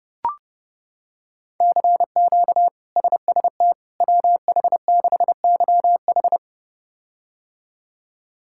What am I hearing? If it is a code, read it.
Morse code, heard as a keyed tone, about 30 wpm, 700 Hz. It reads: CQ SST WH6YH
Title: K1USN SST QSO Practice - 30wpm